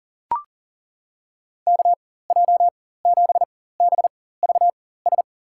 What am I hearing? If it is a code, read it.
Morse code, heard as a keyed tone, about 40 wpm, spaced as for 19 wpm, 700 Hz. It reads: KJ7BVS
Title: USA Call Signs - 6 Characters - 4x Character Spacing - 40wpm